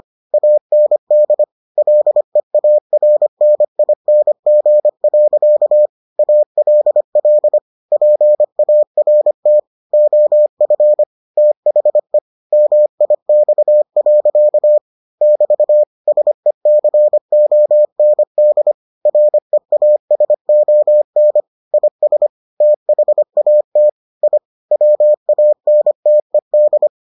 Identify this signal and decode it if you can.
Morse code, heard as a keyed tone, about 25 wpm, 600 Hz. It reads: AND LEARNING. ALL PART OF THE MIX. <BT> SECOND REASON IS THAT I WANTED